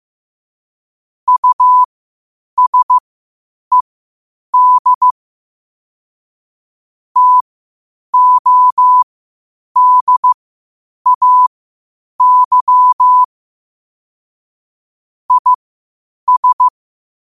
Morse code, heard as a keyed tone, about 15 wpm, spaced as for 8 wpm, 1000 Hz. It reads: USED TODAY IS
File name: qotdfiles / 8wpm